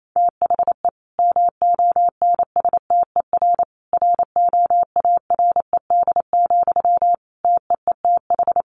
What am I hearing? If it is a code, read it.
Morse code, heard as a keyed tone, about 28 wpm, 700 Hz. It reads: THE MONSTER ROARED, TEETH
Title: A Sound Of Thunder 28wpm